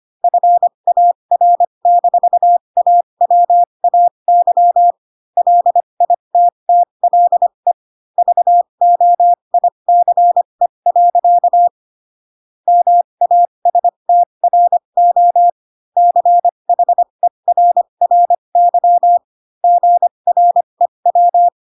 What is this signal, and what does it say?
Morse code, heard as a keyed tone, about 25 wpm, spaced as for 22 wpm, 700 Hz. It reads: FAR-AWAY LITTLE VOICE. MASTRO CHERRY GREW